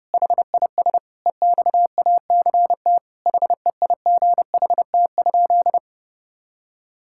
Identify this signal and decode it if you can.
Morse code, heard as a keyed tone, about 30 wpm, 700 Hz. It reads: HIS EXACT HEIGHT?
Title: Sentences from Top 1600 Words 30wpm